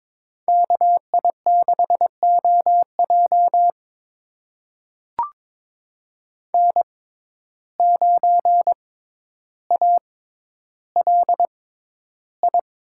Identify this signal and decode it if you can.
Morse code, heard as a keyed tone, about 22 wpm, spaced as for 11 wpm, 700 Hz. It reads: KI6OJ N9ALI
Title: USA Call Signs - 5 Characters - 6x Character Spacing - 22wpm